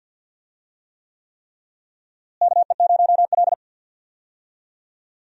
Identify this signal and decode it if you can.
Morse code, heard as a keyed tone, about 53 wpm, 700 Hz. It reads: KE0L
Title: USA Call Signs - 4 Characters - Repeated 3x - 50wpm